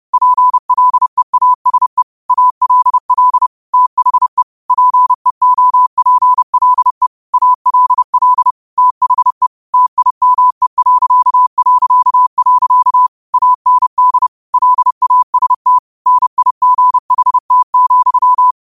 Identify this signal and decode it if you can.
Morse code, heard as a keyed tone, about 30 wpm, 1000 Hz. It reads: PLEASE ALL THE PEOPLE ALL THE TIME... AND LAST NIGHT,
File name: qotdfiles / 30wpm